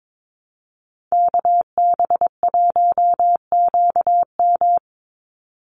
Morse code, heard as a keyed tone, about 22 wpm, 700 Hz. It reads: KB1QM
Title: USA Call Signs 22wpm